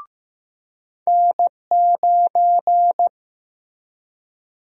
Morse code, heard as a keyed tone, about 15 wpm, 700 Hz. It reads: N9
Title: USA Call Sign Prefixes - 15wpm